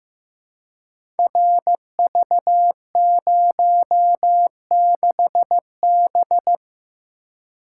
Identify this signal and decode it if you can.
Morse code, heard as a keyed tone, about 15 wpm, 700 Hz. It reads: RV06B
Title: Code Groups - 5 Letter:Numbers - 15wpm